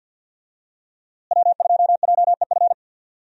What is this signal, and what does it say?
Morse code, heard as a keyed tone, about 51 wpm, 700 Hz. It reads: W2JEF
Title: USA Call Signs - 5 Characters - Repeated 2x - 50wpm